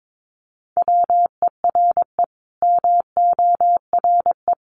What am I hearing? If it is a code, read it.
Morse code, heard as a keyed tone, about 22 wpm, 700 Hz. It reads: WERE MORE
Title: Sets of 5 Words Encore2 22wpm